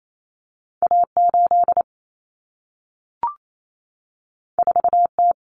Morse code, heard as a keyed tone, about 28 wpm, 700 Hz. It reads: A8 4T
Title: Common Cut Numbers - 2-Digit 28wpm